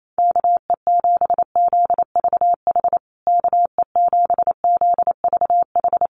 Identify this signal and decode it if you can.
Morse code, heard as a keyed tone, about 28 wpm, 700 Hz. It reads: KE7ZVH KE7ZVH
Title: USA Call Signs Repeated 3x 28wpm